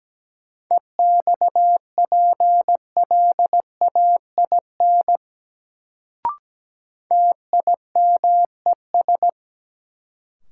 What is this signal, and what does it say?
Morse code, heard as a keyed tone, about 17 wpm, 700 Hz. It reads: EXPLAIN TIMES
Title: Sentences from Top 1100 Words - New Word Drill Alt - 17wpm